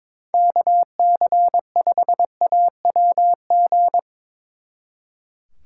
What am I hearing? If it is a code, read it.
Morse code, heard as a keyed tone, about 24 wpm, 700 Hz. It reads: KC5AWG
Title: Speed Racing - USA Call Signs 22wpm